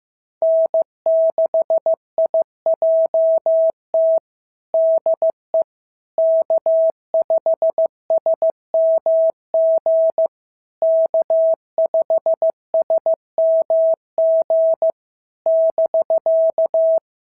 Morse code, heard as a keyed tone, about 14 wpm, 650 Hz. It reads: N6IJT DE K5SMG K5SMG <BK>
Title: SOTA QSO Practice 15wpm